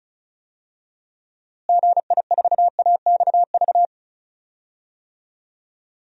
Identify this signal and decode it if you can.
Morse code, heard as a keyed tone, about 36 wpm, 700 Hz. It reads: GI4AXV
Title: DXCC-100 Call Signs - 6 Characters - Repeated 3x - 35wpm